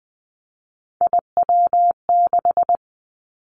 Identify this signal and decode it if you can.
Morse code, heard as a keyed tone, about 20 wpm, 700 Hz. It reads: IW6
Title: Three Challenging Letter-Number Combinations 20wpm